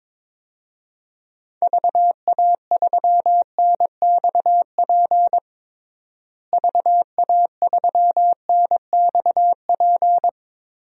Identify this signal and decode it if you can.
Morse code, heard as a keyed tone, about 22 wpm, 700 Hz. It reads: VA3NXP VA3NXP
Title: DXCC-100 Call Signs - 6 Characters - Repeated 2x - 22wpm